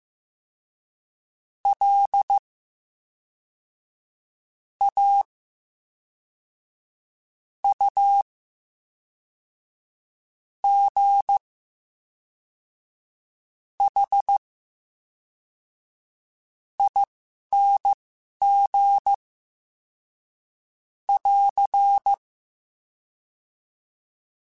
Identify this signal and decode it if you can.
Morse code, heard as a keyed tone, about 15 wpm, spaced as for 4 wpm, 800 Hz. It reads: LAUGHING <AR>
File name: Special INGs / Special INGs-mp3s / VerySlowToFaster Ing @10